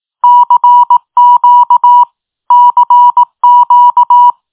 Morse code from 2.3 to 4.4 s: CQ